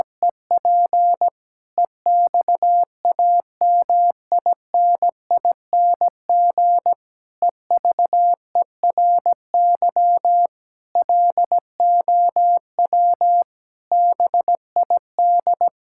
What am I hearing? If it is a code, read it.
Morse code, heard as a keyed tone, about 17 wpm, 700 Hz. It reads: EP EXAMINING EVERY LOW BID
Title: Pangram Challenge v2 - 17wpm